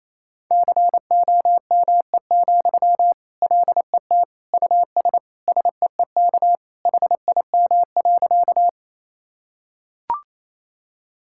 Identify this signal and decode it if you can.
Morse code, heard as a keyed tone, about 28 wpm, 700 Hz. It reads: COME, LET US SEEK HIM.
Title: Sentences from Top 1600 Words 28wpm